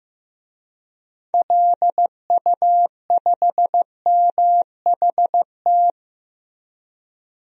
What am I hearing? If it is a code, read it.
Morse code, heard as a keyed tone, about 15 wpm, 700 Hz. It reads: LU5MHT
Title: DXCC-100 Call Signs - 6 Characters - Repeated 2x - 15wpm